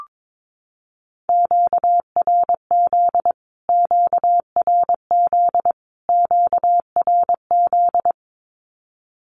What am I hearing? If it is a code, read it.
Morse code, heard as a keyed tone, about 22 wpm, 700 Hz. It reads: QRZ QRZ QRZ
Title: Q-Codes Repeated 3x 22wpm